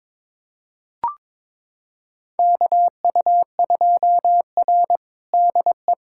Morse code, heard as a keyed tone, about 22 wpm, 700 Hz. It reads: KU2R DE
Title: USA Callsign DE Callsign 22wpm